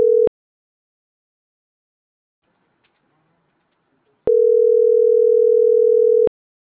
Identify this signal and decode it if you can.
Ringback tone (North America), 440 + 480 Hz.